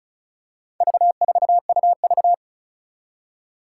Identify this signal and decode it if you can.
Morse code, heard as a keyed tone, about 36 wpm, 700 Hz. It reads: V4UV
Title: Four Challenging Letter-Number Combinations 35wpm